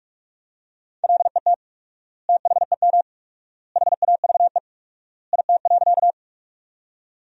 Morse code, heard as a keyed tone, about 46 wpm, 700 Hz. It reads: LET THEM HAVE IT.
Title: 4x Word-Spacing - Sentences from Top 300 Words - 45wpm